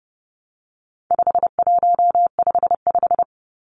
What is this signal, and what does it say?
Morse code, heard as a keyed tone, about 30 wpm, 700 Hz. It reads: 5155
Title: Numbers - 4-Digit 30wpm